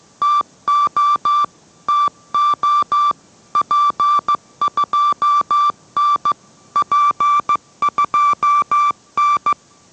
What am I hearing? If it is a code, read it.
Morse code, heard as a keyed tone, about 15 wpm, 1150 Hz. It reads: TO TO P2N P2N